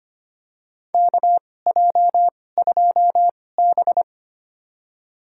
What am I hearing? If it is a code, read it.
Morse code, heard as a keyed tone, about 25 wpm, spaced as for 18 wpm, 700 Hz. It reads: KJ2B
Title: USA Call Signs - 4 Characters - 2x Character Spacing - 25wpm